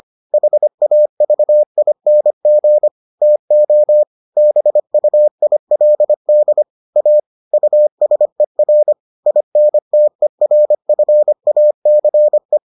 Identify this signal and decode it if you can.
Morse code, heard as a keyed tone, about 25 wpm, 600 Hz. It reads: HAVING TO BUILD A USER INTERFACE